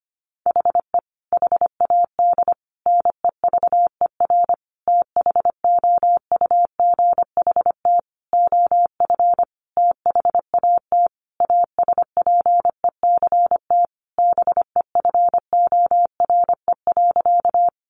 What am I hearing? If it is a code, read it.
Morse code, heard as a keyed tone, about 25 wpm, 700 Hz. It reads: HE HAD NEVER THOUGHT OF THAT ASPECT BEFORE.
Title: Sentences from Top 1300 Words 25wpm